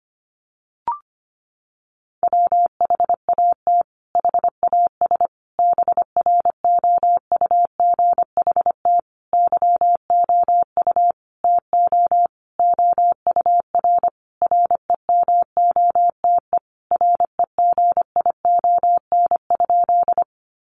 Morse code, heard as a keyed tone, about 25 wpm, 700 Hz. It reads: WHAT HAS BROUGHT YOU TO OUR REMOTE REGION?
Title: Sentences from Top 1500 Words 25wpm